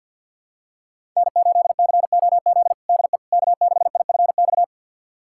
Morse code, heard as a keyed tone, about 52 wpm, 700 Hz. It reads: N8COZ DE K6IFX
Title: USA Callsign DE Callsign - 5 Characters - 50wpm